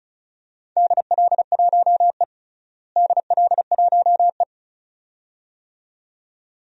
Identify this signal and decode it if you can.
Morse code, heard as a keyed tone, about 34 wpm, 700 Hz. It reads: DL1E DL1E
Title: DXCC-100 Call Signs - 4 Characters - Repeated 2x - 35wpm